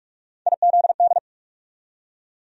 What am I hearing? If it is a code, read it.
Morse code, heard as a keyed tone, about 46 wpm, 700 Hz. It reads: IZD
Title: USA Call Sign Postfixes - 3 Letters 45wpm